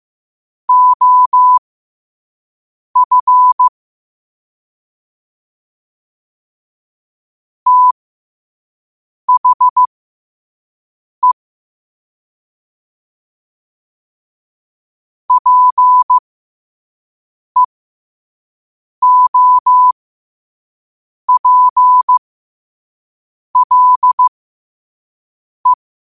Morse code, heard as a keyed tone, about 15 wpm, spaced as for 5 wpm, 1000 Hz. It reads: OF THE PEOPLE
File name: qotdfiles / 5wpm